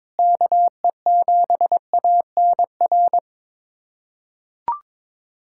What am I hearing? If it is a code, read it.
Morse code, heard as a keyed tone, about 21 wpm, 700 Hz. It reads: KE7ANR
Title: USA Callsign DE Callsign - 6 Characters - 22wpm